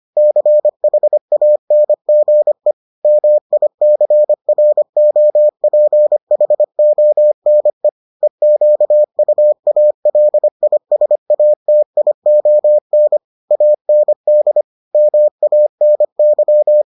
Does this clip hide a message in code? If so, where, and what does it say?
Yes: Morse code from 0.2 to 16.8 s, heard as a keyed tone, about 25 wpm, 600 Hz. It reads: CHANGE MICROPHONE EQUALISATION AND MANY